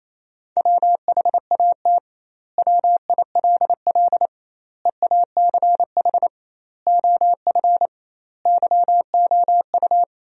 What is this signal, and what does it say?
Morse code, heard as a keyed tone, about 28 wpm, 700 Hz. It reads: WHAT WILL EACH OF YOU
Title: 2x Word-Spacing - Sentences from Top 200 Words - 28wpm